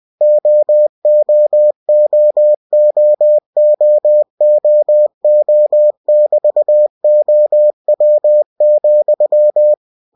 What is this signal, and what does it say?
Morse code, heard as a keyed tone, about 20 wpm, 600 Hz. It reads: OOOOOOO<BT>OW,
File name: cw-books / winnie-the-pooh / winnie-20wpm-page-0070